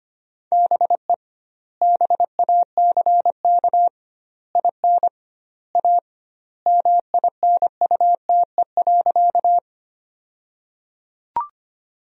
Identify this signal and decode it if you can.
Morse code, heard as a keyed tone, about 25 wpm, 700 Hz. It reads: BE BACK IN A MINUTE.
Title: 2x Word-Spacing - Sentences from Top 500 Words - 25wpm